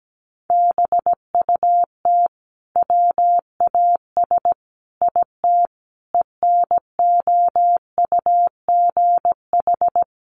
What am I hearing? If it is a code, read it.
Morse code, heard as a keyed tone, about 17 wpm, 700 Hz. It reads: BUT WAS IT ENOUGH
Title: Sentences from Top 1400 Words 17wpm